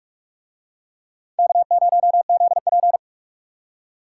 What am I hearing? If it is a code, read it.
Morse code, heard as a keyed tone, about 40 wpm, 700 Hz. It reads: K0ZP